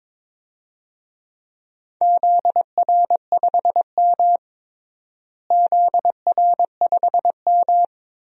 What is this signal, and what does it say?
Morse code, heard as a keyed tone, about 22 wpm, 700 Hz. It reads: ZR5M ZR5M
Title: DXCC-100 Call Signs - 4 Characters - Repeated 3x - 22wpm